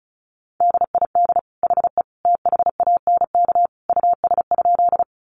Morse code, heard as a keyed tone, about 34 wpm, 700 Hz. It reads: DID HE THANK US?
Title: Sentences from Top 600 Words 35wpm